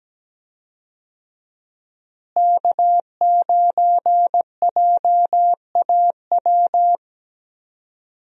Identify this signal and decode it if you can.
Morse code, heard as a keyed tone, about 17 wpm, 700 Hz. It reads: K9JAW
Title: USA Call Signs - 5 Characters - Repeated 3x - 17wpm